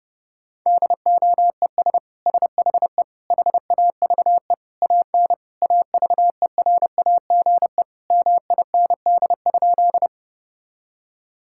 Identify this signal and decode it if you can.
Morse code, heard as a keyed tone, about 30 wpm, 700 Hz. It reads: DOES SHE HAVE AN AVERAGE MIND?
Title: Sentences from Top 1000 Words - Review Part 6 - 30wpm